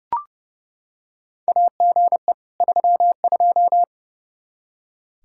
Morse code, heard as a keyed tone, about 30 wpm, 700 Hz. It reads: AGE 32
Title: Practice QSO - Exchange 3 - More Info - 30wpm